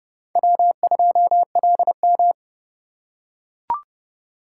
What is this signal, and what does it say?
Morse code, heard as a keyed tone, about 29 wpm, 700 Hz. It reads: W2LM